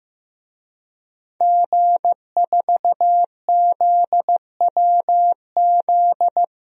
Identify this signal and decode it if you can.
Morse code, heard as a keyed tone, about 15 wpm, 700 Hz. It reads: G4ZWZ
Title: DXCC-100 Call Signs - 5 Characters - 15wpm